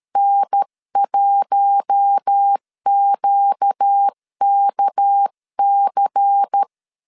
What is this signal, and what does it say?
Morse code, heard as a keyed tone, about 13 wpm, 800 Hz. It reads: N1QKC
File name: CW / DET / 1 6 1500C S C 12wpm 3533